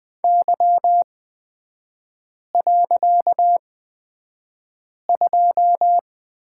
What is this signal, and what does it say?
Morse code, heard as a keyed tone, about 20 wpm, spaced as for 5 wpm, 700 Hz. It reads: Y.2
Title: 255 Lesson 41e - Period - Three Characters 20wpm